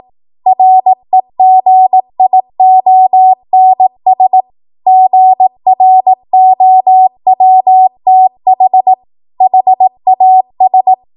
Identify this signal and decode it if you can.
Morse code, heard as a keyed tone, about 18 wpm, 750 Hz. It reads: REGIONS GROWTH HAS
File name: Archive / 18 WPM / 18WPM